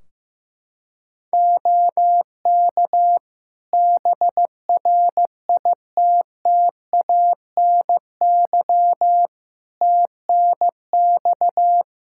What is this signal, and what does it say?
Morse code, heard as a keyed tone, about 15 wpm, 700 Hz. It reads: OK BRITTANY TNX
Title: Practice QSO - Exchange 4 - Ending - 15wpm